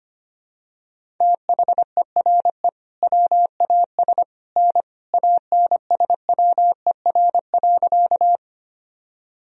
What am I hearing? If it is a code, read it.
Morse code, heard as a keyed tone, about 25 wpm, 700 Hz. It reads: THERE WAS N ANSWER.
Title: Sentences from Top 300 Words and Abbreviations 25wpm